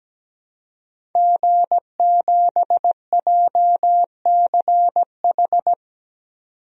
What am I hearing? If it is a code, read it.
Morse code, heard as a keyed tone, about 17 wpm, 700 Hz. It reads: G7JCH
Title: DXCC-100 Call Signs - 5 Characters - 17wpm